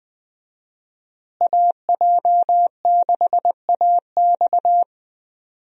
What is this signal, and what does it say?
Morse code, heard as a keyed tone, about 20 wpm, 700 Hz. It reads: AJ6AX
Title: USA Call Signs - 5 Characters - Repeated 3x - 20wpm